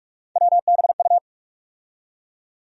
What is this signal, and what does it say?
Morse code, heard as a keyed tone, about 42 wpm, 700 Hz. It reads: WBU
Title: USA Call Sign Postfixes - 3 Letters 45wpm